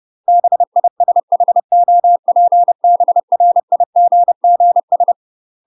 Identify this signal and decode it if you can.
Morse code, heard as a keyed tone, about 30 wpm, 700 Hz. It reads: BISHOPBRIGGS
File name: UK MASTER QSOs TRAINING / 30WPM / UK MASTER QSOs 0009